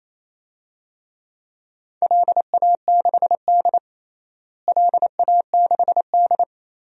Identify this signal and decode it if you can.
Morse code, heard as a keyed tone, about 28 wpm, 700 Hz. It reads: LA6D LA6D
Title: DXCC-100 Call Signs - 4 Characters - Repeated 3x - 28wpm